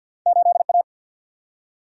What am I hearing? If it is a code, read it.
Morse code, heard as a keyed tone, about 52 wpm, 700 Hz. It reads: 8A
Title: Single-Letter Single-Number 50wpm